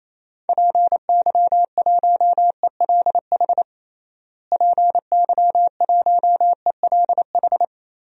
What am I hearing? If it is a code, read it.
Morse code, heard as a keyed tone, about 28 wpm, 700 Hz. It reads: PY1ELH PY1ELH
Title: DXCC-100 Call Signs - 6 Characters - Repeated 3x - 28wpm